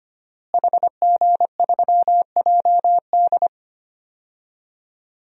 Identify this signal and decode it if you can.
Morse code, heard as a keyed tone, about 25 wpm, 700 Hz. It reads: HG3JD
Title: DXCC-100 Call Signs - 5 Characters - Repeated 3x - 25wpm